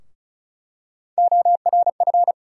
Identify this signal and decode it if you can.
Morse code, heard as a keyed tone, about 34 wpm, 700 Hz. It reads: ORF
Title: USA Call Sign Postfixes - 3 Letters 35wpm